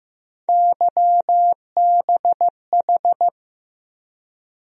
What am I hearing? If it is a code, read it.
Morse code, heard as a keyed tone, about 15 wpm, 700 Hz. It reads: YBH
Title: USA Call Sign Postfixes - 3 Letters 15wpm